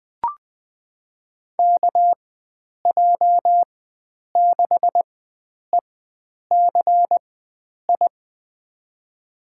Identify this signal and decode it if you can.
Morse code, heard as a keyed tone, about 20 wpm, spaced as for 9 wpm, 700 Hz. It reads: KJ6ECI
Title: USA Call Signs - 6 Characters - 4x Character Spacing - 20wpm